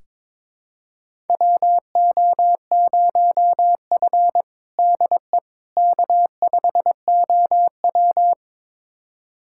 Morse code, heard as a keyed tone, about 22 wpm, 700 Hz. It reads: WO0F DE K5OW
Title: USA Callsign DE Callsign - 4 Characters - 22wpm